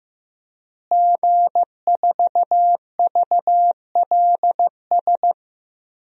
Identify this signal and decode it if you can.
Morse code, heard as a keyed tone, about 15 wpm, 700 Hz. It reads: G4VLS
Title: DXCC-100 Call Signs - 5 Characters - Repeated 2x - 15wpm